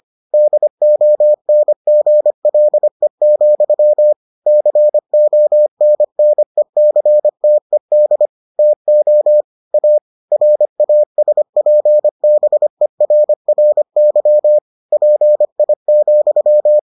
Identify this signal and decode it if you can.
Morse code, heard as a keyed tone, about 25 wpm, 600 Hz. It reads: DONGLE, CONNECTED TO A RASPBERRY PI,